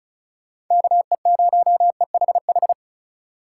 Morse code, heard as a keyed tone, about 44 wpm, 700 Hz. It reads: KE0EHH